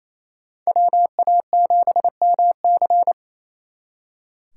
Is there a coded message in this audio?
Yes: Morse code from 0.7 to 3.1 s, heard as a keyed tone, about 35 wpm, 700 Hz. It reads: WA7MC